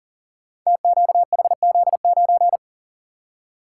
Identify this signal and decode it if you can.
Morse code, heard as a keyed tone, about 42 wpm, 690 Hz. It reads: TQHZ9
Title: Code Groups - 5 Letter:Numbers - 40wpm